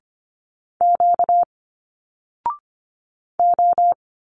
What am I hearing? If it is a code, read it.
Morse code, heard as a keyed tone, about 25 wpm, 700 Hz. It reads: Q O
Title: Single Letters 25wpm